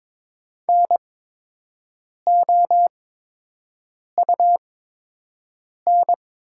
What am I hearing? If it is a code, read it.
Morse code, heard as a keyed tone, about 22 wpm, spaced as for 6 wpm, 700 Hz. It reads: NOUN
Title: 8x Character Spacing - Top 500 Words - 22wpm